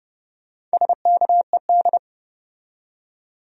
Morse code, heard as a keyed tone, about 33 wpm, 700 Hz. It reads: SKED